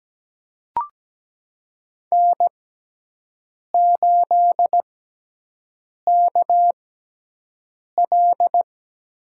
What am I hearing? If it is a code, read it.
Morse code, heard as a keyed tone, about 17 wpm, spaced as for 6 wpm, 700 Hz. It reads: N8KL
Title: USA Call Signs - 6x Character Spacing - 17wpm